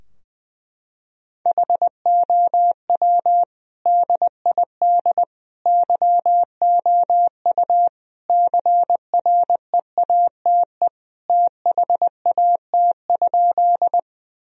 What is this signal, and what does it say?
Morse code, heard as a keyed tone, about 20 wpm, 700 Hz. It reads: HOW DID YOU CREATE THAT?
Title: Sentences from Top 1000 Words - Review Part 3 - 20wpm